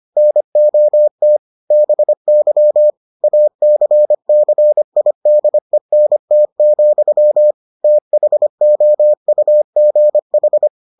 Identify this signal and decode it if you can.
Morse code, heard as a keyed tone, about 25 wpm, 600 Hz. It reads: NOT BY ACCIDENT, THOUGH